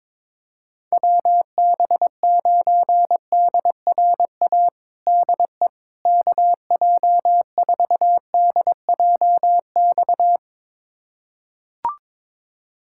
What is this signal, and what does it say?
Morse code, heard as a keyed tone, about 22 wpm, 700 Hz. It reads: WB9DRA DE KJ4DJX
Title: USA Callsign DE Callsign - 6 Characters - 22wpm